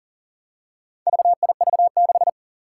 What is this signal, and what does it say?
Morse code, heard as a keyed tone, about 42 wpm, 700 Hz. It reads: VIV6
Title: Four Challenging Letter-Number Combinations 40wpm